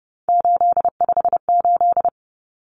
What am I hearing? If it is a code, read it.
Morse code, heard as a keyed tone, about 30 wpm, 700 Hz. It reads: 858
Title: Numbers - 3-Digit 30wpm